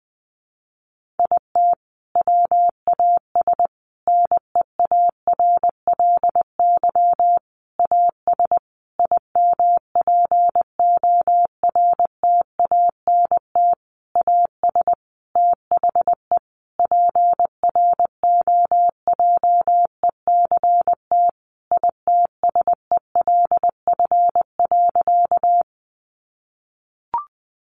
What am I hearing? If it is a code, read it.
Morse code, heard as a keyed tone, about 19 wpm, 700 Hz. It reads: IT WAS NEARLY AS IMPORTANT AS THE PROJECT ITSELF.
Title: Sentences from Top 1400 Words 20wpm